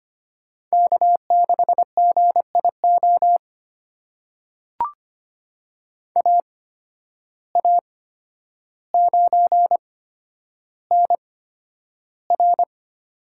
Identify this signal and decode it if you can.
Morse code, heard as a keyed tone, about 25 wpm, spaced as for 11 wpm, 700 Hz. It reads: K6GIO AA9NR
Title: USA Call Signs - 5 Characters - 8x Character Spacing - 25wpm